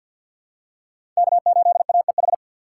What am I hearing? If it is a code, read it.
Morse code, heard as a keyed tone, about 48 wpm, 700 Hz. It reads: K8AEH
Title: USA Call Signs - 5 Characters - 3x Character Spacing - 50wpm